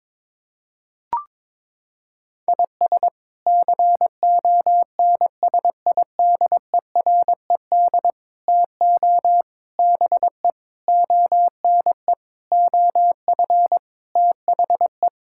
Morse code, heard as a keyed tone, about 22 wpm, 700 Hz. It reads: IS CONSIDERED TO BE ONE OF THE
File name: practice / 22wpm / Sets of 8 Words - 22wpm